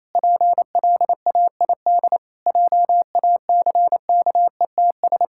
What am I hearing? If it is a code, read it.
Morse code, heard as a keyed tone, about 28 wpm, 700 Hz. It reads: PLAID JACKETS
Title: Pangram Challenge v2 - 28wpm